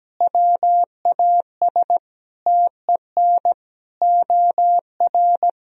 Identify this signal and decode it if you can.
Morse code, heard as a keyed tone, about 17 wpm, 700 Hz. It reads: WAS TEN OR
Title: Sentences from Top 1000 Words - Review Part 3 - 17wpm